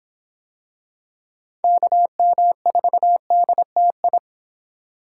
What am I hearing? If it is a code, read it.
Morse code, heard as a keyed tone, about 25 wpm, 700 Hz. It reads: KM4DTI